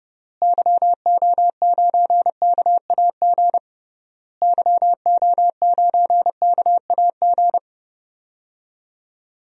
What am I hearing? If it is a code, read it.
Morse code, heard as a keyed tone, about 30 wpm, 700 Hz. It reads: YO9KAG YO9KAG
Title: DXCC-100 Call Signs - 6 Characters - Repeated 3x - 30wpm